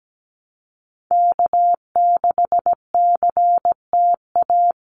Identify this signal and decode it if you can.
Morse code, heard as a keyed tone, about 17 wpm, 700 Hz. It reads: K6CTA
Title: Call Signs Repeated 2x - 17wpm